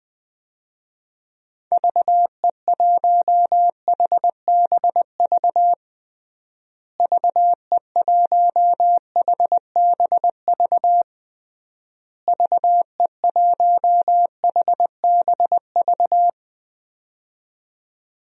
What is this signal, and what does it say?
Morse code, heard as a keyed tone, about 20 wpm, 700 Hz. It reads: VE1HBV VE1HBV VE1HBV
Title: DXCC-100 Call Signs - 6 Characters - Repeated 3x - 20wpm